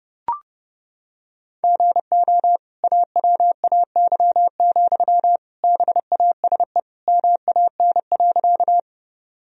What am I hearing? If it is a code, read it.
Morse code, heard as a keyed tone, about 30 wpm, 700 Hz. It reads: GO AWAY, BASE MAN.
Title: Sentences from Top 1000 Words - Review Part 1 - 30wpm